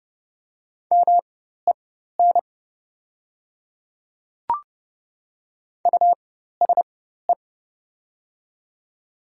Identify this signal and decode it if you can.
Morse code, heard as a keyed tone, about 30 wpm, spaced as for 14 wpm, 700 Hz. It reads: MEN USE
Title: 4x Character Spacing - Top 200 Words - 30wpm